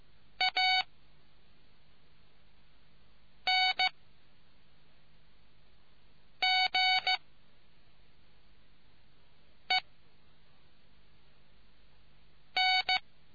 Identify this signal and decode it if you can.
Morse code, heard as a keyed tone, about 15 wpm, 750 Hz. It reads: A N G E N